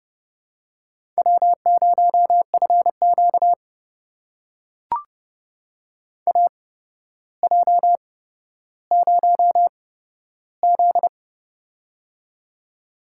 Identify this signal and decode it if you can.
Morse code, heard as a keyed tone, about 30 wpm, spaced as for 13 wpm, 700 Hz. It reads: W0FQ AJ0Z
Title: USA Call Signs - 4 Characters - 8x Character Spacing - 30wpm